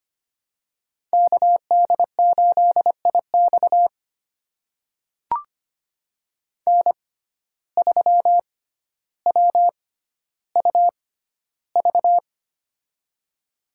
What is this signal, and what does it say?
Morse code, heard as a keyed tone, about 25 wpm, spaced as for 13 wpm, 700 Hz. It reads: KD8IX N3WUV
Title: USA Call Signs - 5 Characters - 6x Character Spacing - 25wpm